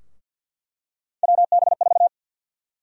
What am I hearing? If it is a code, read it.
Morse code, heard as a keyed tone, about 52 wpm, 700 Hz. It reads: WB4
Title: USA Call Sign Prefixes - Repeated 2x - 50wpm